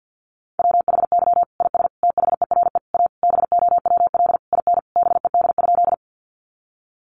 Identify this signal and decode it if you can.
Morse code, heard as a keyed tone, about 50 wpm, 700 Hz. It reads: WHY IS THERE A BOWL IN BED?
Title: Sentences from Top 1200 Words 50wpm